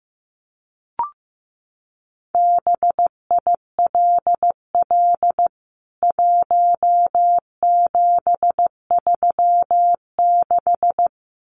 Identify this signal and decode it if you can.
Morse code, heard as a keyed tone, about 15 wpm, 700 Hz. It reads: BILL 1736
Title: CWT Exchange 15wpm